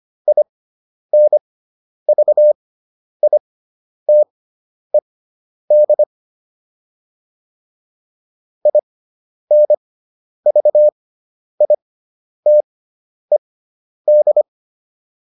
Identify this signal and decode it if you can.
Morse code, heard as a keyed tone, about 25 wpm, spaced as for 10 wpm, 600 Hz. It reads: INVITED INVITED